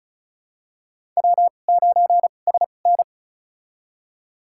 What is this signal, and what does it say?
Morse code, heard as a keyed tone, about 34 wpm, spaced as for 25 wpm, 700 Hz. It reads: W9SN
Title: USA Call Signs - 4 Characters - 2x Character Spacing - 35wpm